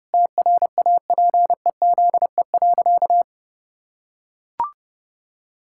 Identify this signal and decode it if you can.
Morse code, heard as a keyed tone, about 30 wpm, 700 Hz. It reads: TRAPEZE.
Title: Pangram Challenge v2 - 30wpm